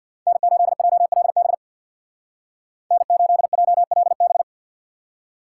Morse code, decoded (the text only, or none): N8JLB N8JLB